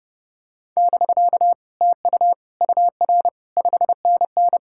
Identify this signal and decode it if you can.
Morse code, heard as a keyed tone, about 32 wpm, 700 Hz. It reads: <BK> TU UR 5NN